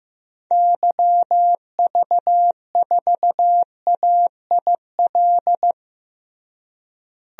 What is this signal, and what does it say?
Morse code, heard as a keyed tone, about 16 wpm, 700 Hz. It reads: YV4AIL
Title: DXCC-100 Call Signs - 6 Characters - Repeated 2x - 15wpm